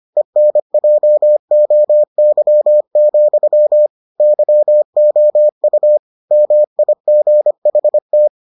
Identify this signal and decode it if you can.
Morse code, heard as a keyed tone, about 25 wpm, 600 Hz. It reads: ENJOY, YOU MIGHT